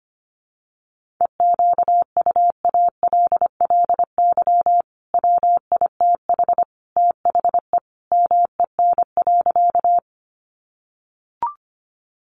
Morse code, heard as a keyed tone, about 25 wpm, 700 Hz. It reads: EQUALLY WITH THE MEN.
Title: Sentences from Top 1400 Words 25wpm